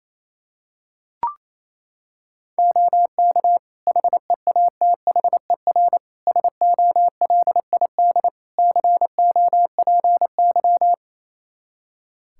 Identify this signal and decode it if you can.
Morse code, heard as a keyed tone, about 27 wpm, 700 Hz. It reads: OK HEATHER SOLID COPY
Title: K1USN SST QSO Practice 28wpm